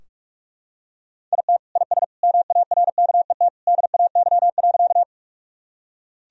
Morse code, heard as a keyed tone, about 42 wpm, 700 Hz. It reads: IT IS MARKET DAY.
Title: Sentences from Top 1000 Words - Review Part 3 - 45wpm